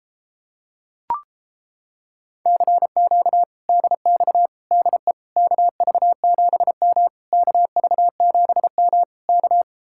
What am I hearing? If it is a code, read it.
Morse code, heard as a keyed tone, about 29 wpm, 700 Hz. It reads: CQ DX DE KV7M KV7M K